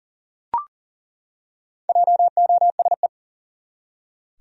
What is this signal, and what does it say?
Morse code, heard as a keyed tone, about 40 wpm, 700 Hz. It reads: JOSE
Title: USA Common Names - Max 5 Letters - 40wpm